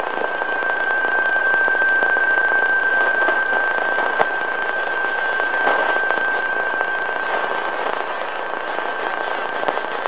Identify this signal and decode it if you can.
PSK31, about 31.25 baud.